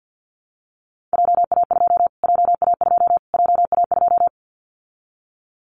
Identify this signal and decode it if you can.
Morse code, heard as a keyed tone, about 50 wpm, 700 Hz. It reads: JA2 JA2 JA2
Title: Callsign Prefixes for Top 25 DXCC - Repeated 3x - 50wpm